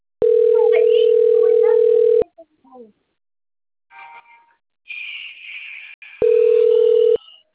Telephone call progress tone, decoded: ringback tone

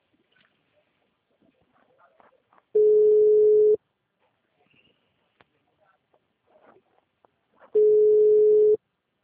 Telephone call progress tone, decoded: ringback tone

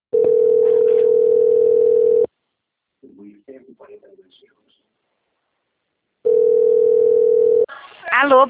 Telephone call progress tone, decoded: ringback tone

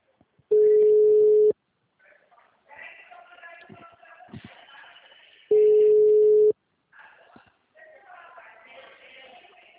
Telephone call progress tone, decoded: ringback tone

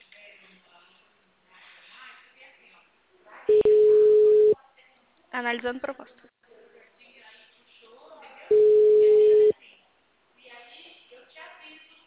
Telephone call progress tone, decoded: ringback tone

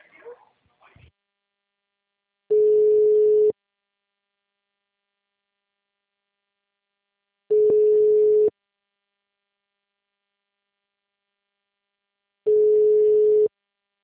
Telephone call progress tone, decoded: ringback tone

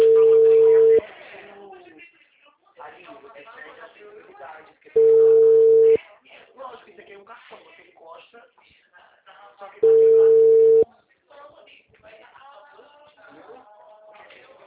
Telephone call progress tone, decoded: ringback tone

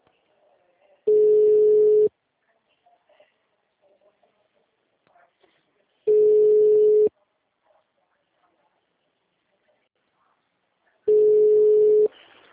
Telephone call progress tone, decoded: ringback tone

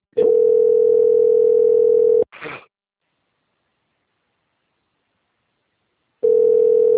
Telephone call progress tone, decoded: ringback tone

